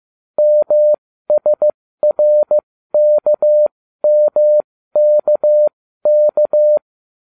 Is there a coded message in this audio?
Yes: Morse code from 0.4 to 6.8 s, heard as a keyed tone, about 15 wpm, spaced as for 13 wpm, 600 Hz. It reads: MSRKMKK